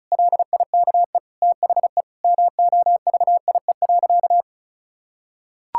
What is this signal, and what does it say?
Morse code, heard as a keyed tone, about 36 wpm, 700 Hz. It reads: LIKE THE MOVIE.
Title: Sentences from Top 1000 Words - Review Part 5 - 35wpm